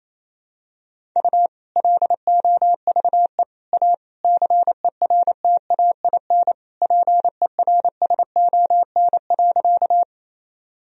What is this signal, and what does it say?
Morse code, heard as a keyed tone, about 28 wpm, 700 Hz. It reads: U LOVE A CERTAIN PERSON.
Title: Sentences from Top 500 Words and Abbreviations 28wpm